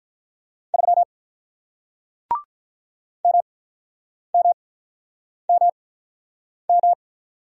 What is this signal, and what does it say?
Morse code, heard as a keyed tone, about 46 wpm, spaced as for 9 wpm, 700 Hz. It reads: 3 MMMM